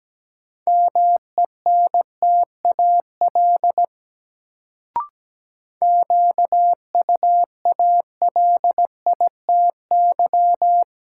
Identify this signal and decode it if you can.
Morse code, heard as a keyed tone, about 17 wpm, 700 Hz. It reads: MENTAL QUALITY
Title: Sentences from Top 1000 Words - Review Part 4 - New Word Drill - 17wpm